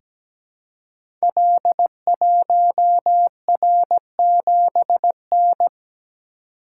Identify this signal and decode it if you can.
Morse code, heard as a keyed tone, about 17 wpm, 690 Hz. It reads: L1R7N
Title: Code Groups - 5 Letter:Numbers - 17wpm